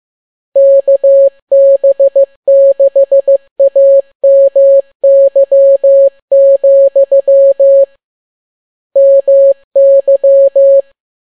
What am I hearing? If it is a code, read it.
Morse code, heard as a keyed tone, about 15 wpm, 550 Hz. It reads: KB6AMY, MY